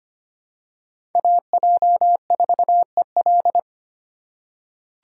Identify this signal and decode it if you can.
Morse code, heard as a keyed tone, about 25 wpm, 700 Hz. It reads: AJ4EL
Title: USA Call Signs - 5 Characters - 4x Character Spacing - 25wpm